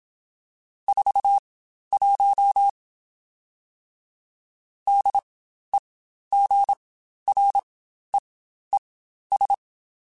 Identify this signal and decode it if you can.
Morse code, heard as a keyed tone, about 27 wpm, spaced as for 12 wpm, 800 Hz. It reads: 41 DEGREES
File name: Spaced Characters / Spaced Characters-mp3s / Weather4 14 sp @25